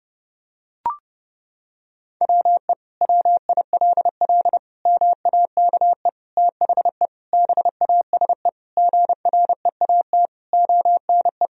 Morse code, heard as a keyed tone, about 30 wpm, 700 Hz. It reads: WE WILL MAKE THE BASE GREAT ONE
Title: Sentences from Top 1000 Words - Review Part 1 - 30wpm